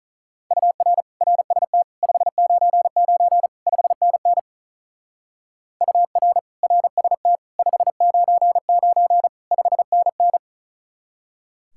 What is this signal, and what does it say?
Morse code, heard as a keyed tone, about 44 wpm, 700 Hz. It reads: UR RST 599 5NN UR RST 599 5NN